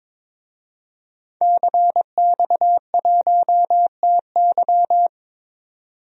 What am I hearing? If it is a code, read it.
Morse code, heard as a keyed tone, about 22 wpm, 700 Hz. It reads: CX1TY
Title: DXCC-100 Call Signs - 5 Characters - 22wpm